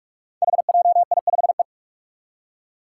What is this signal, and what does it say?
Morse code, heard as a keyed tone, about 44 wpm, 700 Hz. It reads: HJI5E
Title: Five Challenging Letter-Number Combinations 45wpm